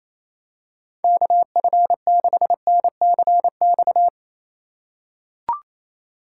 Morse code, heard as a keyed tone, about 28 wpm, 700 Hz. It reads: KF6NCX
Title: K1USN SST QSO Practice - 28wpm